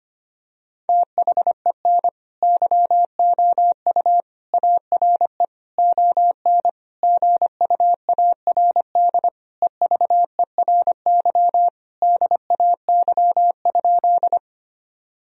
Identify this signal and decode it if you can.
Morse code, heard as a keyed tone, about 25 wpm, 700 Hz. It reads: THEN YOU ARE ON GUARD EVERY DAY?
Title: Sentences from Top 1000 Words - Review Part 4 - 25wpm